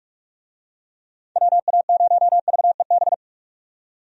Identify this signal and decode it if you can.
Morse code, heard as a keyed tone, about 46 wpm, 700 Hz. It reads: WA0VEB
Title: USA Call Signs - 6 Characters - 45wpm